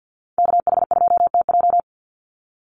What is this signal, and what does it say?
Morse code, heard as a keyed tone, about 50 wpm, 700 Hz. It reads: KH2TJ
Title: NA SOTA Chaser Callsigns 50wpm